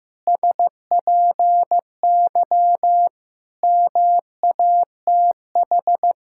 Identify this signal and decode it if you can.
Morse code, heard as a keyed tone, about 15 wpm, 700 Hz. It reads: SPY MATH
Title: Pangram Challenge v2 - 15wpm